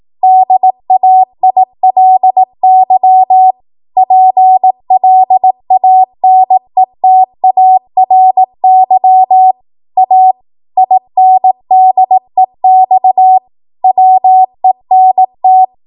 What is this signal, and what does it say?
Morse code, heard as a keyed tone, about 18 wpm, 750 Hz. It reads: DAILY PLANETARY A INDEX WENT
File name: Archive / 18 WPM / 18WPM